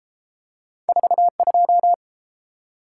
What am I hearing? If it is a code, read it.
Morse code, heard as a keyed tone, about 30 wpm, 700 Hz. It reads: 42